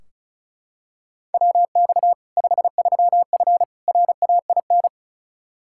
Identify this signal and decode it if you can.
Morse code, heard as a keyed tone, about 34 wpm, 700 Hz. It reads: WX 53F RAIN
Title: Practice QSO - Exchange 2 - Info - 35wpm